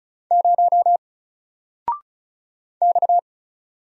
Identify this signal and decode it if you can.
Morse code, heard as a keyed tone, about 36 wpm, 700 Hz. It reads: 0 X
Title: Single Letter-Number 35wpm